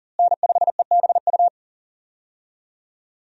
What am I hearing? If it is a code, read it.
Morse code, heard as a keyed tone, about 40 wpm, 700 Hz. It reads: N5EBU
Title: USA Call Signs - 5 Characters - Repeated 3x - 40wpm